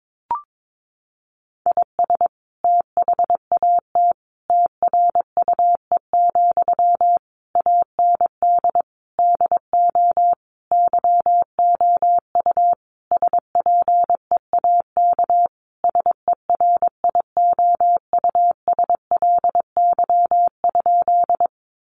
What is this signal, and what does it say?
Morse code, heard as a keyed tone, about 22 wpm, 700 Hz. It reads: IS THAT TRUE, AND DO YOU SPEAK SERIOUSLY?
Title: Sentences from Top 1200 Words 22wpm